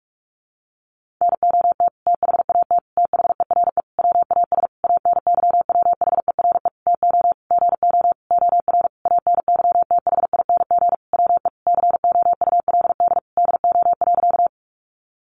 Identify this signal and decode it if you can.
Morse code, heard as a keyed tone, about 44 wpm, 700 Hz. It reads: NOT THAT THERE WAS ANYWHERE TO GO OR ANYTHING WE COULD DO.
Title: Sentences from Top 1400 Words 45wpm